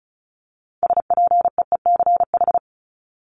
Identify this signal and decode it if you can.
Morse code, heard as a keyed tone, about 34 wpm, 700 Hz. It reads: SPEECH
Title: Top 1751-2000 Words 35wpm